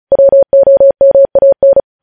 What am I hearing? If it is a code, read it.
Morse code, heard as a keyed tone, about 36 wpm, 550 Hz. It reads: WOMAN
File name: podcast / News headlines-1x-35x35